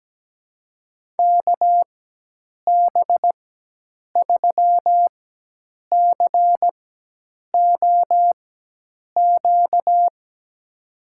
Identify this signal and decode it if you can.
Morse code, heard as a keyed tone, about 17 wpm, spaced as for 8 wpm, 700 Hz. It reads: KB3COQ
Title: USA Call Signs - 4x Character Spacing - 17wpm